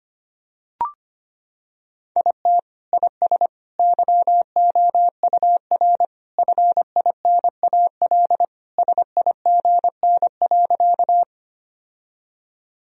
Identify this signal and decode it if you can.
Morse code, heard as a keyed tone, about 25 wpm, 700 Hz. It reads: IT IS YOUR FINAL SIGN.
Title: Sentences from Top 1000 Words - Review Part 2 - 25wpm